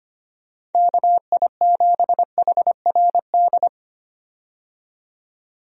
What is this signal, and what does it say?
Morse code, heard as a keyed tone, about 25 wpm, 700 Hz. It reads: KI7HRD
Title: USA Call Signs - 6 Characters - 25wpm